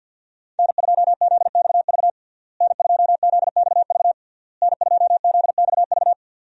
Morse code, heard as a keyed tone, about 52 wpm, 700 Hz. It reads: N2ZXV N2ZXV N2ZXV
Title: USA Call Signs - 5 Characters - Repeated 3x - 50wpm